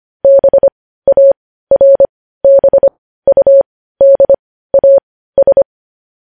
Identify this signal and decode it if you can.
Morse code, heard as a keyed tone, about 25 wpm, spaced as for 15 wpm, 550 Hz. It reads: BARBUDAS
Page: News Headlines in Morse Code at 15 WPM